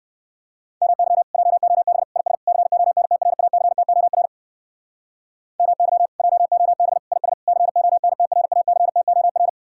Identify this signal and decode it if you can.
Morse code, heard as a keyed tone, about 59 wpm, 700 Hz. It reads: MY JOB IS CONTRACTOR MY JOB IS CONTRACTOR